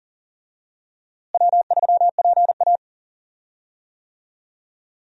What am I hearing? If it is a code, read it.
Morse code, heard as a keyed tone, about 40 wpm, 700 Hz. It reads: W3PA
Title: USA Call Signs - 4 Characters - Repeated 3x - 40wpm